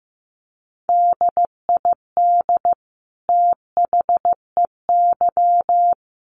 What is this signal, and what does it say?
Morse code, heard as a keyed tone, about 15 wpm, 700 Hz. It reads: DID THEY
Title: Sentences from Top 1400 Words 15wpm